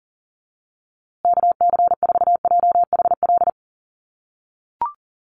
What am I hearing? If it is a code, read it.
Morse code, heard as a keyed tone, about 40 wpm, 700 Hz. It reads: KC4JHL
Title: USA Call Signs Repeated 2x 40wpm